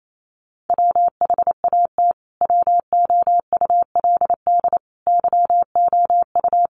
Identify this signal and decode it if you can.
Morse code, heard as a keyed tone, about 28 wpm, 700 Hz. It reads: WHAT WOULD YOU
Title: Sentences from Top 200 Words 28wpm